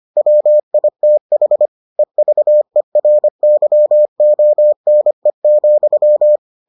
Morse code, heard as a keyed tone, about 25 wpm, 600 Hz. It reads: WITH EVERYONE,